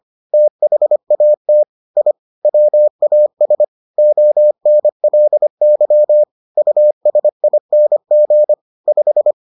Morse code, heard as a keyed tone, about 25 wpm, 600 Hz. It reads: THAT I WAS ONLY USING 5